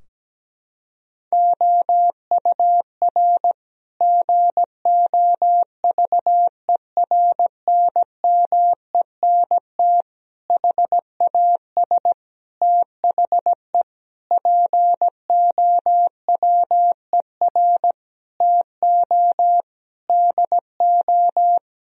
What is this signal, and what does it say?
Morse code, heard as a keyed tone, about 17 wpm, 700 Hz. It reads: OUR GOVERNMENT HAS THE POWER TO DO
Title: Sentences from Top 1000 Words - Review Part 3 - 17wpm